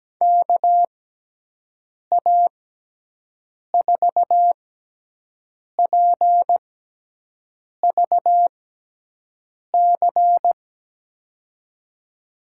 Morse code, heard as a keyed tone, about 17 wpm, spaced as for 6 wpm, 700 Hz. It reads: KA4PVC
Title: USA Call Signs - 6 Characters - 6x Character Spacing - 17wpm